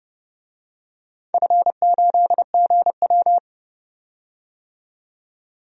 Morse code, heard as a keyed tone, about 30 wpm, 700 Hz. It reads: F8GW
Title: DXCC-100 Call Signs - 4 Characters - 30wpm